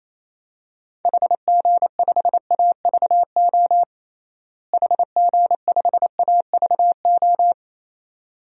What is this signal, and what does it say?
Morse code, heard as a keyed tone, about 28 wpm, 700 Hz. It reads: HG5AVO HG5AVO
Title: DXCC-100 Call Signs - 6 Characters - Repeated 2x - 28wpm